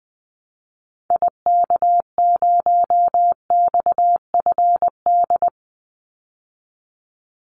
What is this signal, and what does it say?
Morse code, heard as a keyed tone, about 20 wpm, 700 Hz. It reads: IK0XFD
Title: Call Signs Repeated 3x - 20wpm